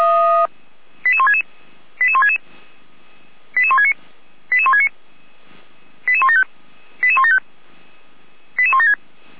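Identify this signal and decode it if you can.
Five-tone selective call (ZVEI), heard as quick runs of five tones, tones of about 70 ms. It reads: 88166 88166 88169 88169 88175 88175 88176
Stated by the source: ZVEI five-tone selective call received by SDR.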